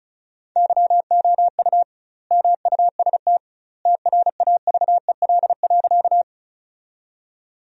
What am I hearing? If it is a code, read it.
Morse code, heard as a keyed tone, about 36 wpm, 700 Hz. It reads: YOU MUST TRAVEL.
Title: 2x Word-Spacing - Sentences from Top 400 Words - 35wpm